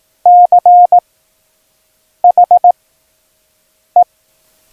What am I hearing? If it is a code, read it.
Morse code, heard as a keyed tone, about 18 wpm, 710 Hz. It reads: C H E
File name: acs / Instrument Rating / Instrument ACS